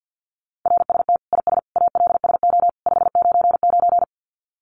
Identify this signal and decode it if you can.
Morse code, heard as a keyed tone, about 50 wpm, 700 Hz. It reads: RST IS ALSO 599
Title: Common QSO Phrases 50wpm